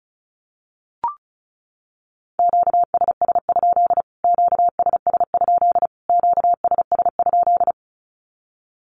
Morse code, heard as a keyed tone, about 34 wpm, 700 Hz. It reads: QSS? QSS? QSS?
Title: Q-Codes Repeated 3x 35wpm